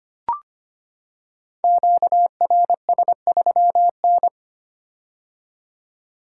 Morse code, heard as a keyed tone, about 25 wpm, 700 Hz. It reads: QRS3N
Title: Code Groups - 5 Letter:Numbers - 25wpm